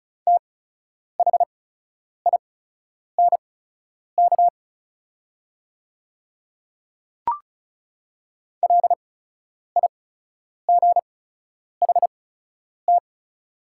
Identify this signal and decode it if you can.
Morse code, heard as a keyed tone, about 34 wpm, spaced as for 10 wpm, 700 Hz. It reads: THINK LIGHT
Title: 8x Character Spacing - Top 300 Words - 35wpm